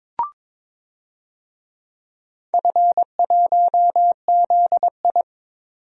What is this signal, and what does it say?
Morse code, heard as a keyed tone, about 23 wpm, 700 Hz. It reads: F1ZI
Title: DXCC-100 Call Signs - 4 Characters - 22wpm